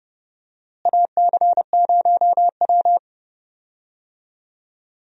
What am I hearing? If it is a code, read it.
Morse code, heard as a keyed tone, about 31 wpm, 700 Hz. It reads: AC0W
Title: USA Call Signs - 4 Characters - Repeated 3x - 30wpm